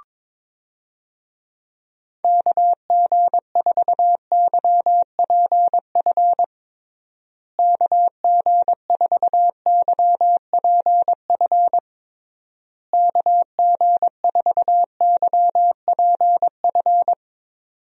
Morse code, heard as a keyed tone, about 22 wpm, 700 Hz. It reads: KG4YPF KG4YPF KG4YPF
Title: USA Call Signs - 6 Characters - Repeated 3x - 22wpm